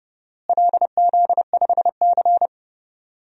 Morse code, heard as a keyed tone, about 31 wpm, 700 Hz. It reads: LZ5C
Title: DXCC-100 Call Signs - 4 Characters - Repeated 3x - 30wpm